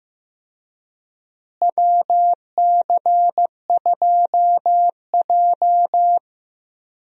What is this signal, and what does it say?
Morse code, heard as a keyed tone, about 15 wpm, 700 Hz. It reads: WC2J
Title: USA Call Signs - 4 Characters - Repeated 3x - 15wpm